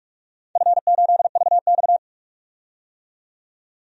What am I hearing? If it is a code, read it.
Morse code, heard as a keyed tone, about 47 wpm, 700 Hz. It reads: F8VX
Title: DXCC-100 Call Signs - 4 Characters - 45wpm